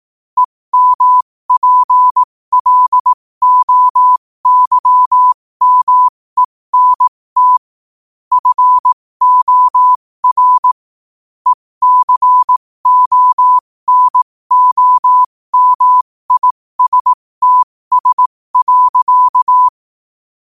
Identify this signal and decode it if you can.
Morse code, heard as a keyed tone, about 18 wpm, spaced as for 15 wpm, 1000 Hz. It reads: EMPLOYMENT FOR ECONOMISTS.
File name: qotdfiles / 15wpm